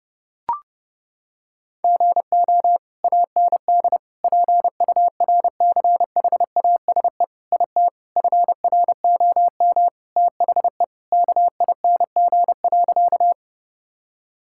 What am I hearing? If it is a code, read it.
Morse code, heard as a keyed tone, about 30 wpm, 700 Hz. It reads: GO AND PURCHASE IT FROM THE KING.
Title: Sentences from Top 1000 Words - Review Part 5 - 30wpm